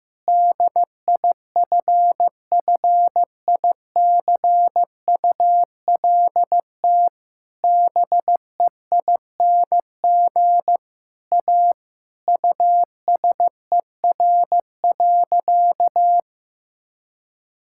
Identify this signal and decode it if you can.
Morse code, heard as a keyed tone, about 16 wpm, 700 Hz. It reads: DIFFICULT BEING A USER.
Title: Sentences from Top 1000 Words - Review Part 5 - 15wpm